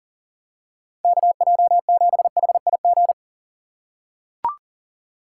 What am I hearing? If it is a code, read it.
Morse code, heard as a keyed tone, about 40 wpm, 700 Hz. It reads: KJ7HIG
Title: USA Call Signs - 6 Characters - 4x Character Spacing - 40wpm